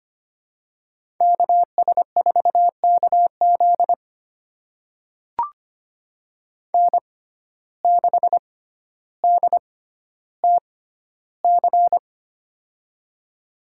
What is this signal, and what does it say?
Morse code, heard as a keyed tone, about 25 wpm, spaced as for 13 wpm, 700 Hz. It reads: KS4KZ N6DTC
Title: USA Call Signs - 6x Character Spacing - 25wpm